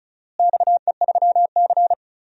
Morse code, heard as a keyed tone, about 36 wpm, 700 Hz. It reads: XE3C
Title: DXCC-100 Call Signs - 4 Characters - Repeated 2x - 35wpm